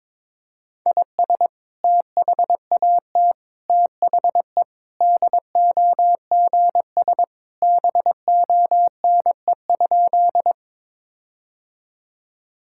Morse code, heard as a keyed tone, about 22 wpm, 700 Hz. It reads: IS THAT THE DOGS BONE?
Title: Sentences from Top 1600 Words 22wpm